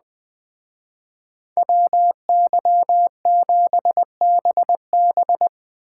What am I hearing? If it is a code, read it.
Morse code, heard as a keyed tone, about 20 wpm, 700 Hz. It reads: WY7BB
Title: Speed Racing - USA 2x2 Call Signs 17wpm